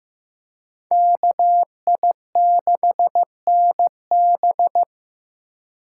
Morse code, heard as a keyed tone, about 15 wpm, 700 Hz. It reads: KI6NB
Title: USA Call Signs - 5 Characters - 2x Character Spacing - 15wpm